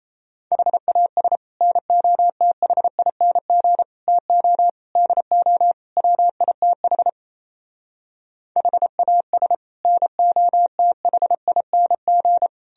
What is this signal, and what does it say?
Morse code, heard as a keyed tone, about 36 wpm, 700 Hz. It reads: HAS NOTHING TO DO WITH HAS NOTHING